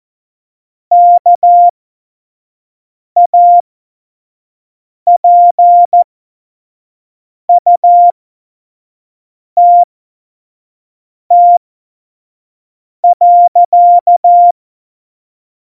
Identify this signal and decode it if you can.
Morse code, heard as a keyed tone, about 14 wpm, spaced as for 5 wpm, 700 Hz. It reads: KAPUTT.
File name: Morsekurs / Witze / Witz 26. Februar 2019, 25 F70